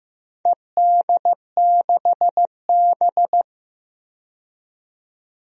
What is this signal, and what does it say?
Morse code, heard as a keyed tone, about 15 wpm, 700 Hz. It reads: ED6B
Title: DXCC-100 Call Signs - 4 Characters - 15wpm